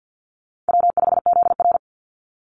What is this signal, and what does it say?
Morse code, heard as a keyed tone, about 49 wpm, 700 Hz. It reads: W5ZR
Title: Easy Call Signs Repeated 3x - 50wpm